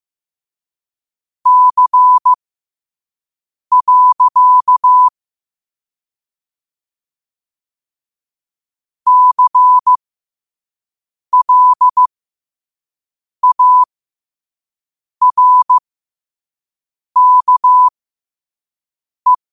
Morse code, heard as a keyed tone, about 15 wpm, spaced as for 5 wpm, 1000 Hz. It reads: C. CLARKE